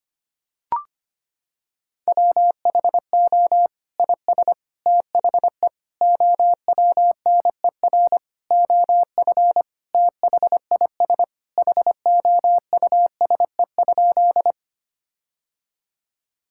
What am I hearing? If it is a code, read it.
Morse code, heard as a keyed tone, about 25 wpm, 700 Hz. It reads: WHO IS THE OWNER OF THIS HOUSE?
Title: Sentences from Top 1600 Words 25wpm